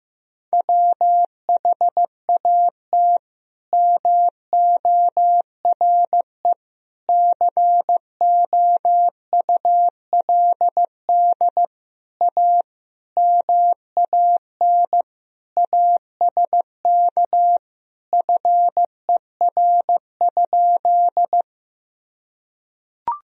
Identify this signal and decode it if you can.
Morse code, heard as a keyed tone, about 16 wpm, 700 Hz. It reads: WHAT MORE COULD A MAN ASK FER?
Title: Sentences from Top 200 Words and Abbreviations 15wpm